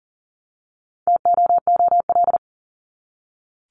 Morse code, heard as a keyed tone, about 40 wpm, 700 Hz. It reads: TOOL
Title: Top 751-1000 Words 40wpm